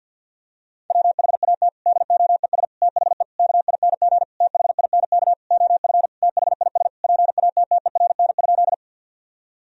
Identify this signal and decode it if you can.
Morse code, heard as a keyed tone, about 53 wpm, 700 Hz. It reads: WHAT DOES THE KING THINK OF THIS PATTERN?
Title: Sentences from Top 1000 Words - Review Part 2 - 50wpm